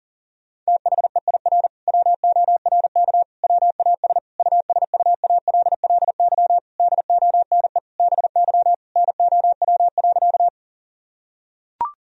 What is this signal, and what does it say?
Morse code, heard as a keyed tone, about 37 wpm, 700 Hz. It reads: THEIR WORK WAS USUALLY DONE BY NOW.
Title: Sentences from Top 1000 Words - Review Part 3 - 40wpm